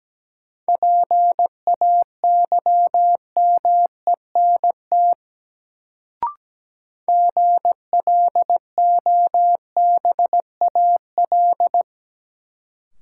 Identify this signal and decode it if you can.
Morse code, heard as a keyed tone, about 17 wpm, 700 Hz. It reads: PAYMENT GLOBAL
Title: Sentences from Top 1200 Words - New Word Drill Alt - 17wpm